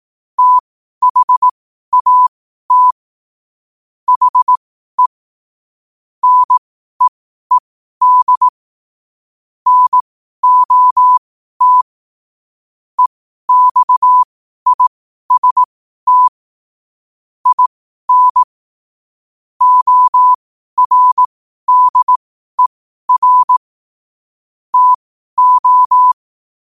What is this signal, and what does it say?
Morse code, heard as a keyed tone, about 18 wpm, spaced as for 12 wpm, 1000 Hz. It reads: THAT HE NEED NOT EXIST IN ORDER TO
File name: qotdfiles / 12wpm